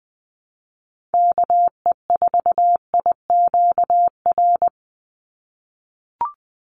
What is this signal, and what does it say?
Morse code, heard as a keyed tone, about 20 wpm, 700 Hz. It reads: KE4IQR
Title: USA Call Signs 20wpm